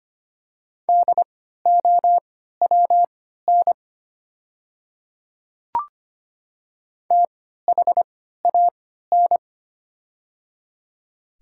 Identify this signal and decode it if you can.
Morse code, heard as a keyed tone, about 25 wpm, spaced as for 14 wpm, 700 Hz. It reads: DOWN THAN
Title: 3x Character Spacing - Top 100 Words - 25wpm